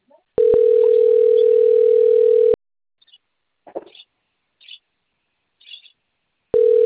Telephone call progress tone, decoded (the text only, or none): ringback tone